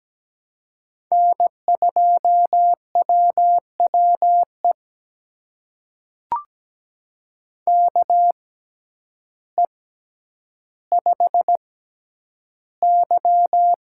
Morse code, heard as a keyed tone, about 17 wpm, spaced as for 9 wpm, 700 Hz. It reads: N2WWE KE5Y
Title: USA Call Signs - 6x Character Spacing - 17wpm